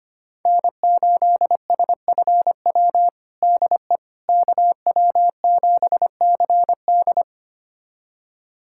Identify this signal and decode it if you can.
Morse code, heard as a keyed tone, about 25 wpm, 700 Hz. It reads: N8SFW DE KW7CD
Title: USA Callsign DE Callsign - 5 Characters - 25wpm